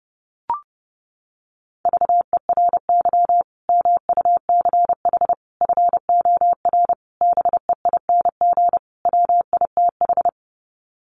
Morse code, heard as a keyed tone, about 30 wpm, 700 Hz. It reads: VERY MUCH FOR BEING WITH
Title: Sets of 5 Words 30wpm